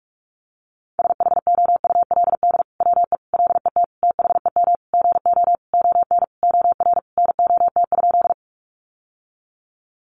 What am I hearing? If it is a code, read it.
Morse code, heard as a keyed tone, about 44 wpm, 700 Hz. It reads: SHOULD WE LET THEM GO ON OR NOT?
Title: Sentences from Top 600 Words 45wpm